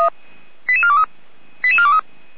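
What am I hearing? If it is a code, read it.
Five-tone selective call (ZVEI), heard as quick runs of five tones, tones of about 80 ms. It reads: 88423 88423
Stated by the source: ZVEI five-tone selective call 88327 received with an SDR.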